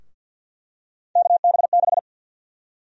Morse code, heard as a keyed tone, about 52 wpm, 700 Hz. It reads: KB6
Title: USA Call Sign Prefixes - Repeated 2x - 50wpm